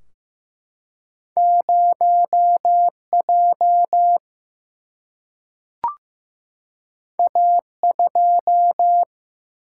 Morse code, heard as a keyed tone, about 15 wpm, 700 Hz. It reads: 0J A2
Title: Single-Letter Single-Number 15wpm